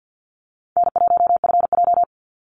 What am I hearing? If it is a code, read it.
Morse code, heard as a keyed tone, about 50 wpm, 700 Hz. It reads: N1FJ
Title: NA SOTA Activator Callsigns 50wpm